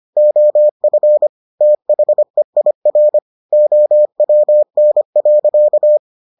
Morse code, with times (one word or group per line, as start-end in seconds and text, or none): 0.2-1.3 OF
1.6-3.2 THEIR
3.5-6.0 OWN.